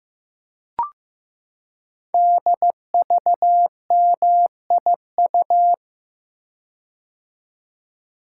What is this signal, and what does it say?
Morse code, heard as a keyed tone, about 15 wpm, 700 Hz. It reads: DVMIU
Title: Code Groups - 5 Letters - 15wpm